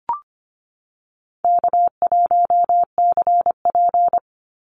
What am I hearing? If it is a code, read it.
Morse code, heard as a keyed tone, about 25 wpm, 700 Hz. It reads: K1CP
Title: Call Signs Repeated 2x - 25wpm